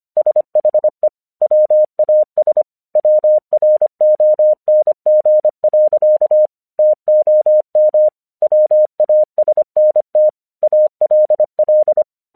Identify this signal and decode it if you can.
Morse code, heard as a keyed tone, about 25 wpm, 610 Hz. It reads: SHE WAS WRONG. TOM WASNT ALL
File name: 1 English 25wpm / AQC08